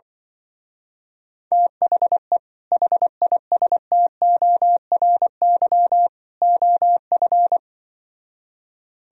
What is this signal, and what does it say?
Morse code, heard as a keyed tone, about 24 wpm, 700 Hz. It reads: THE HISTORY OF
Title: Speed-Racing - Sets of 3 Words 20wpm